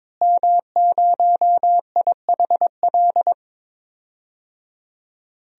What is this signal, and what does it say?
Morse code, heard as a keyed tone, about 22 wpm, 700 Hz. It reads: M0IHL
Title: DXCC-100 Call Signs - 5 Characters - Repeated 3x - 22wpm